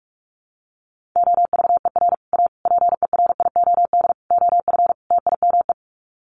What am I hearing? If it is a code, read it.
Morse code, heard as a keyed tone, about 44 wpm, 700 Hz. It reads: OVER A PERIOD OF TIME
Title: Sets of 5 Words Encore2 45wpm